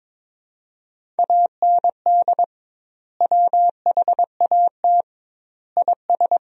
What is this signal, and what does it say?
Morse code, heard as a keyed tone, about 22 wpm, 700 Hz. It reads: AND WHAT IS
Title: 2x Word-Spacing - Sentences from Top 100 Words - 22wpm